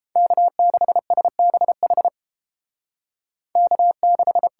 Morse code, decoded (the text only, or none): K6SBH K6